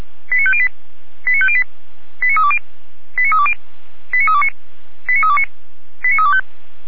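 Five-tone selective call (ZVEI), heard as quick runs of five tones, tones of about 70 ms. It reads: 89508 89508 89320 89320 89329 89329 89325
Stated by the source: Received on SDR, ZVEI five-tone call 88074.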